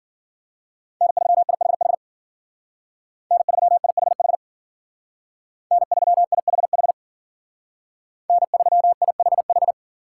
Morse code, heard as a keyed tone, about 48 wpm, 700 Hz. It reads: N3IHH N3IHH N3IHH N3IHH
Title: Speed Racing - USA 1x3 Call Signs 40wpm